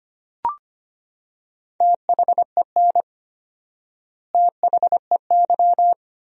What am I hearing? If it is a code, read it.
Morse code, heard as a keyed tone, about 25 wpm, 700 Hz. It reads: THEN THEY
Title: 4x Word-Spacing - Sentences from Top 200 Words - 25wpm